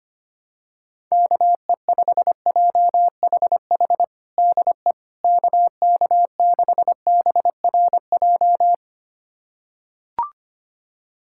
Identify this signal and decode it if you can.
Morse code, heard as a keyed tone, about 25 wpm, 700 Hz. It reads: KE5JHH DE KK6BRJ
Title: USA Callsign DE Callsign - 6 Characters - 25wpm